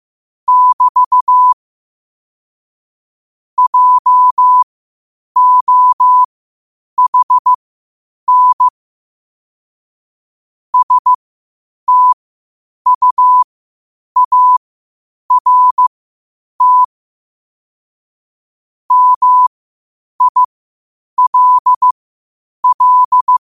Morse code, heard as a keyed tone, about 15 wpm, spaced as for 8 wpm, 1000 Hz. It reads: <BT> JOHN STUART MILL
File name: qotdfiles / 8wpm